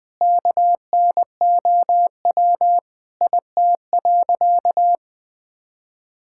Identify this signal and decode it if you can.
Morse code, heard as a keyed tone, about 19 wpm, 700 Hz. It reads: KNOW IT.
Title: Sentences with 9-Word Phrases - 20wpm